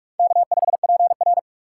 Morse code, heard as a keyed tone, about 44 wpm, 700 Hz. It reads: K5PR
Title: USA Call Signs - 4 Characters - 6x Character Spacing - 45wpm